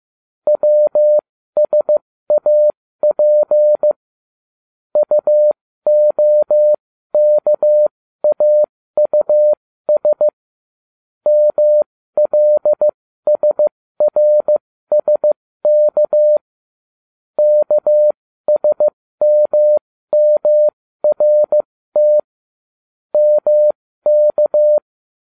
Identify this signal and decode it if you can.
Morse code, heard as a keyed tone, about 15 wpm, spaced as for 13 wpm, 600 Hz. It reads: WSAP UOKAUS MLSRSK KSMMRT MK